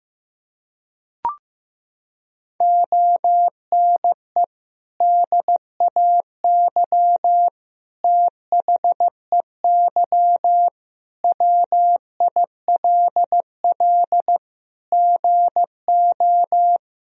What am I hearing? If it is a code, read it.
Morse code, heard as a keyed tone, about 15 wpm, 700 Hz. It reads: ONE DAY THEY WILL GO
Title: Sentences from Top 1000 Words - Review Part 1 - 15wpm